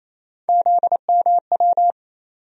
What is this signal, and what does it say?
Morse code, heard as a keyed tone, about 28 wpm, 700 Hz. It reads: ZMW
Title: USA Call Sign Postfixes - 3 Letters 28wpm